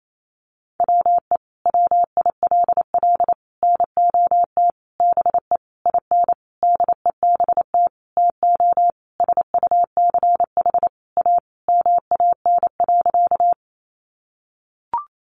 Morse code, heard as a keyed tone, about 28 wpm, 700 Hz. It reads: WE WILL NOT BE IN DEBT TO SUCH A MAN.
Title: Sentences from Top 1300 Words 28wpm